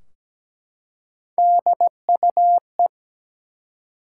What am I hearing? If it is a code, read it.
Morse code, heard as a keyed tone, about 17 wpm, 700 Hz. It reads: DUE
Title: Three Letter Words - 17wpm